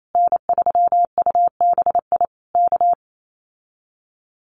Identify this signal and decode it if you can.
Morse code, heard as a keyed tone, about 28 wpm, 700 Hz. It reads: N3UBI K
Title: Common QSO Phrases Encore 28wpm